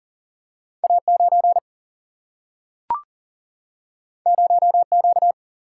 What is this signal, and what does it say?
Morse code, heard as a keyed tone, about 41 wpm, 700 Hz. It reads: A9 0Q
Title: Single-Letter Single-Number 40wpm